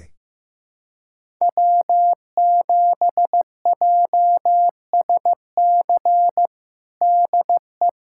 Morse code, heard as a keyed tone, about 15 wpm, 700 Hz. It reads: W7JSC DE
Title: USA Callsign DE Callsign - 5 Characters - 15wpm